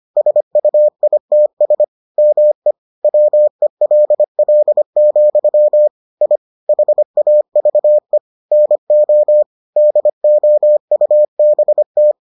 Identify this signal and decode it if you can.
Morse code, heard as a keyed tone, about 25 wpm, 600 Hz. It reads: SUITS ME WELL, I HAVE NO DOUBT